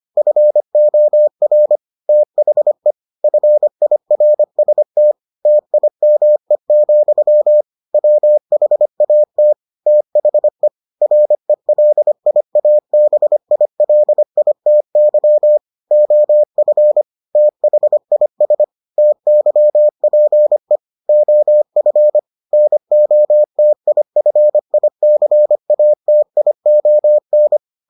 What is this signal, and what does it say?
Morse code, heard as a keyed tone, about 25 wpm, 600 Hz. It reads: FOR THE FIRST TIME, WHAT THE RELIABILITY OF THIS TYPE OF NOTIFICATION